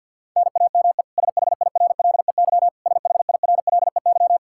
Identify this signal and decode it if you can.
Morse code, heard as a keyed tone, about 52 wpm, 700 Hz. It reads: NAME SHIRLEY SHIRLEY
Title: K1USN SST QSO Practice 50wpm